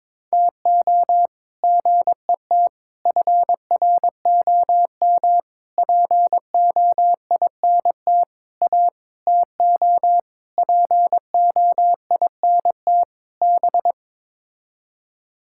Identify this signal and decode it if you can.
Morse code, heard as a keyed tone, about 22 wpm, 700 Hz. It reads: TO GET FROM POINT A TO POINT B
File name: practice / 22wpm / Sets of 8 Words - 22wpm